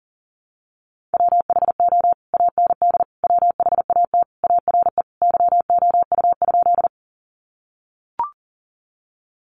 Morse code, heard as a keyed tone, about 37 wpm, 700 Hz. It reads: WHO AND WHAT ARE YOU?
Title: Sentences from Top 100 Words 40wpm